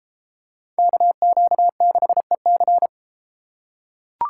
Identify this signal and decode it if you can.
Morse code, heard as a keyed tone, about 32 wpm, 700 Hz. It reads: KQ6EC